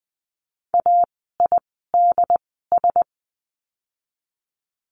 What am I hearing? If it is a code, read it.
Morse code, heard as a keyed tone, about 20 wpm, spaced as for 15 wpm, 700 Hz. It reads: AIDS
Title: 068 Lesson 12f - Letter d - Words 20wpm